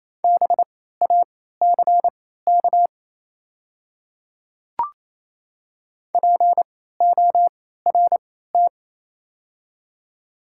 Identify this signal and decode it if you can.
Morse code, heard as a keyed tone, about 28 wpm, spaced as for 16 wpm, 700 Hz. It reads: BACK PORT
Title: 3x Character Spacing - Top 200 Words - 28wpm